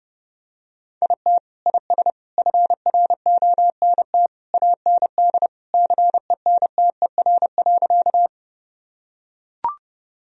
Morse code, heard as a keyed tone, about 30 wpm, 700 Hz. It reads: IT IS FRONT AND CENTER.
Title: Sentences from Top 1000 Words - Review Part 2 - 30wpm